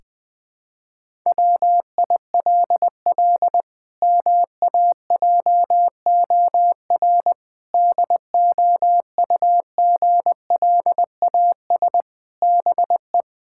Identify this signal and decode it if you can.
Morse code, heard as a keyed tone, about 20 wpm, 700 Hz. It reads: WILL MAJOR DOUGLAS BE
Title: Pangram Challenge v2 - 20wpm